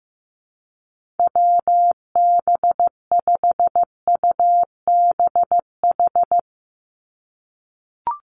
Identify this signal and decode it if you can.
Morse code, heard as a keyed tone, about 15 wpm, 700 Hz. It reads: WB5UBH
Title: USA Call Signs 15wpm